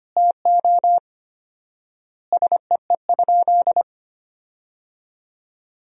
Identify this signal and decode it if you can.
Morse code, heard as a keyed tone, about 25 wpm, 700 Hz. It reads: TO SEE?
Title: 4x Word-Spacing - Sentences from Top 100 Words - 25wpm